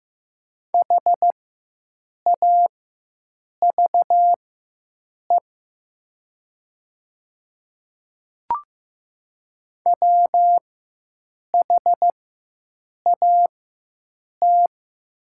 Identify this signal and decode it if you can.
Morse code, heard as a keyed tone, about 15 wpm, spaced as for 7 wpm, 700 Hz. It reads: HAVE WHAT
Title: 4x Character Spacing - Top 100 Words - 15wpm